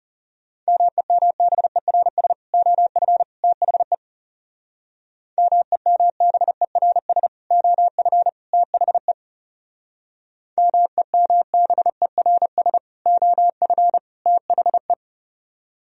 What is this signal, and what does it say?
Morse code, heard as a keyed tone, about 39 wpm, 700 Hz. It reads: MEMBERS OF THE MEMBERS OF THE MEMBERS OF THE